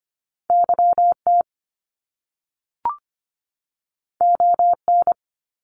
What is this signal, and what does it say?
Morse code, heard as a keyed tone, about 25 wpm, 700 Hz. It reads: YT ON
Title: Canadian Provinces and Territories 25wpm